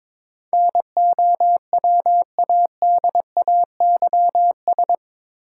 Morse code, heard as a keyed tone, about 21 wpm, 700 Hz. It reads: NOWADAYS
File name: practice / 22wpm / Eight Letter Words - 22wpm